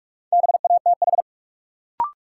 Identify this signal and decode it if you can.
Morse code, heard as a keyed tone, about 46 wpm, 700 Hz. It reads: BATH
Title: Sentences from Top 1600 Words - New Word Drill Alt - 45wpm